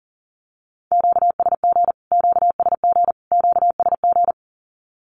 Morse code, heard as a keyed tone, about 40 wpm, 700 Hz. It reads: QSG QSG QSG
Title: Q-Codes Repeated 3x 40wpm